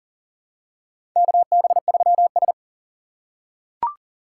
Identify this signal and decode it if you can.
Morse code, heard as a keyed tone, about 40 wpm, 700 Hz. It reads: KB3S
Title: USA Call Signs - 4 Characters - 6x Character Spacing - 40wpm